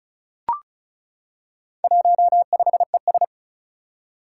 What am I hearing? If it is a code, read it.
Morse code, heard as a keyed tone, about 36 wpm, 700 Hz. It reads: 15ES
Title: Four Challenging Letter-Number Combinations 35wpm